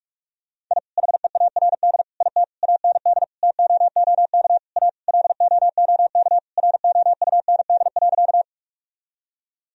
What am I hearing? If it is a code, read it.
Morse code, heard as a keyed tone, about 46 wpm, 700 Hz. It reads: I HEARD IT AND TOOK A LOOK ROUND.
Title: Sentences from Top 1000 Words - Review Part 1 - 45wpm